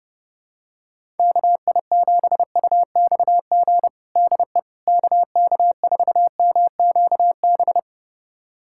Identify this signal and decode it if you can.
Morse code, heard as a keyed tone, about 30 wpm, 700 Hz. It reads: KI7UXG DE KK4MQB
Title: USA Callsign DE Callsign - 6 Characters - 30wpm